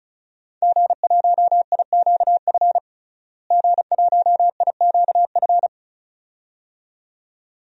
Morse code, heard as a keyed tone, about 34 wpm, 700 Hz. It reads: G1IQF G1IQF
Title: DXCC-100 Call Signs - 5 Characters - Repeated 2x - 35wpm